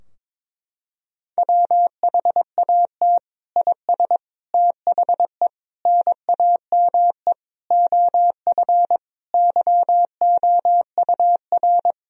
Morse code, heard as a keyed tone, about 21 wpm, 700 Hz. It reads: WHAT IS THE NAME OF YOUR
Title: Sentences from Top 1000 Words - Review Part 2 - 22wpm